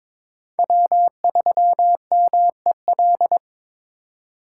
Morse code, heard as a keyed tone, about 22 wpm, 700 Hz. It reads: W3MEL
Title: K1USN SST QSO Practice - 22wpm